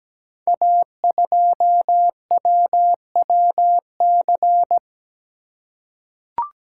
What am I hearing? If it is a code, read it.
Morse code, heard as a keyed tone, about 17 wpm, 700 Hz. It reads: A2WWC
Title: Code Groups - 5 Letter:Numbers - 17wpm